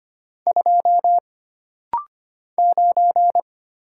Morse code, heard as a keyed tone, about 25 wpm, 700 Hz. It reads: 2 9
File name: Single Character (Letter:Number:Punctuation:Prosign) - Rapid-Fire - 25wpm